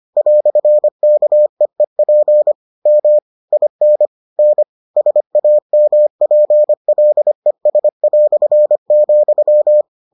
Morse code, heard as a keyed tone, about 25 wpm, 600 Hz. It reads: "KEEP M IN N SAMPLES",